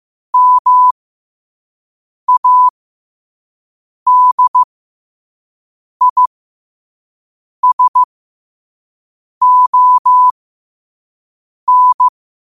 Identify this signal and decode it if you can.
Morse code, heard as a keyed tone, about 15 wpm, spaced as for 5 wpm, 1000 Hz. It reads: MADISON
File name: qotdfiles / 5wpm